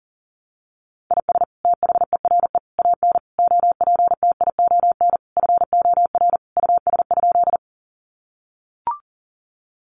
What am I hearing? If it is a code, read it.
Morse code, heard as a keyed tone, about 40 wpm, 700 Hz. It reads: IS THERE AN OPTION FOR US?
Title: Sentences from Top 1300 Words 40wpm